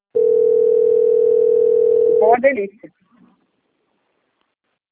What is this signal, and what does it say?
Ringback tone (North America), 440 + 480 Hz, with speech.
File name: out- - - - - .19325